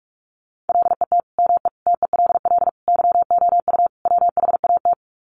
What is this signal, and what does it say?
Morse code, heard as a keyed tone, about 44 wpm, 700 Hz. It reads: LET ME TELL YOU WHAT
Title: Sets of 5 Words Encore 45wpm